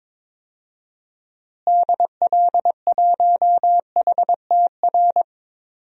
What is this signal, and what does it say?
Morse code, heard as a keyed tone, about 22 wpm, 700 Hz. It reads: DL1HTR
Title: DXCC-100 Call Signs - 6 Characters - Repeated 2x - 22wpm